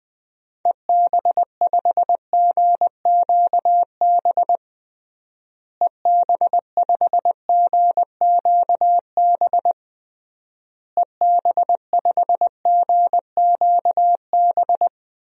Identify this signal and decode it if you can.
Morse code, heard as a keyed tone, about 21 wpm, 700 Hz. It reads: EB5GQB EB5GQB EB5GQB
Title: DXCC-100 Call Signs - 6 Characters - Repeated 3x - 20wpm